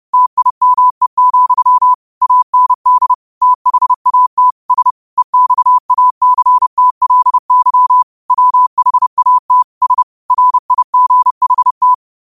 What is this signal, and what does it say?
Morse code, heard as a keyed tone, about 30 wpm, 1000 Hz. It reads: TIME, AND THAT S EXACTLY WHAT S RIGHT